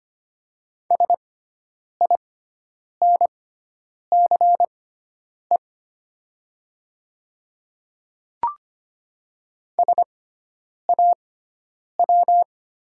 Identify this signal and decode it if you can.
Morse code, heard as a keyed tone, about 25 wpm, spaced as for 9 wpm, 700 Hz. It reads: SINCE SAW
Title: 6x Character Spacing - Top 300 Words - 25wpm